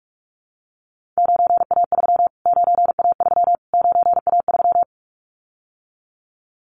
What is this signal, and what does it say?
Morse code, heard as a keyed tone, about 44 wpm, 700 Hz. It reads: 9A3 9A3 9A3
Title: Callsign Prefixes for Top 25 DXCC - Repeated 3x - 45wpm